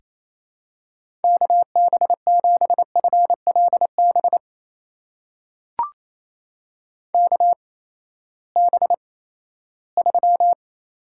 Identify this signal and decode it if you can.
Morse code, heard as a keyed tone, about 28 wpm, spaced as for 16 wpm, 700 Hz. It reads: KB7FLB KB3
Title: USA Call Signs - 6 Characters - 8x Character Spacing - 28wpm